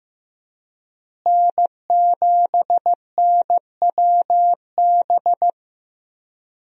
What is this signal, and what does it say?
Morse code, heard as a keyed tone, about 15 wpm, 700 Hz. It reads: N7NWB
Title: USA Call Signs - 5 Characters - Repeated 2x - 15wpm